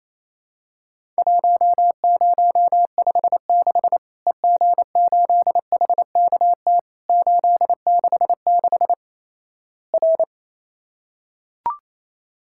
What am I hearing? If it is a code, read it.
Morse code, heard as a keyed tone, about 28 wpm, 700 Hz. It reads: 1056 EG8HKT 866 R
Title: Worked All Europe DX Contest - QTC Exchange - 28wpm